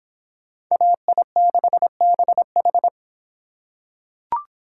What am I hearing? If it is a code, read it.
Morse code, heard as a keyed tone, about 25 wpm, 700 Hz. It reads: AI6BH